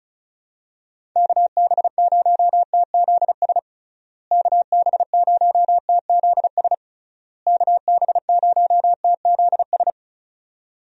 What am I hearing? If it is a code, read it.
Morse code, heard as a keyed tone, about 36 wpm, 700 Hz. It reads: KB0TZS KB0TZS KB0TZS
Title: USA Call Signs - 6 Characters - Repeated 3x - 35wpm